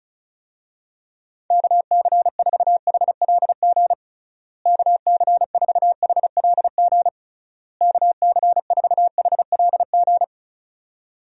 Morse code, heard as a keyed tone, about 36 wpm, 700 Hz. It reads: KC4HLG KC4HLG KC4HLG
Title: USA Call Signs - 6 Characters - Repeated 3x - 35wpm